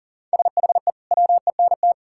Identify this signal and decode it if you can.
Morse code, heard as a keyed tone, about 37 wpm, 700 Hz. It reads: SHE WENT
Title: Sentences from Top 1600 Words 40wpm